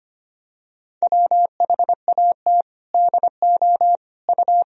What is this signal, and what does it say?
Morse code, heard as a keyed tone, about 25 wpm, 700 Hz. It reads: WHAT DO U
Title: Sentences from Top 100 Words and Abbreviations 25wpm